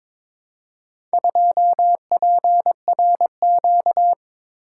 Morse code, heard as a keyed tone, about 22 wpm, 700 Hz. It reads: 2PRQ
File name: practice / 22wpm / Four Letter-Number Combinations - 22wpm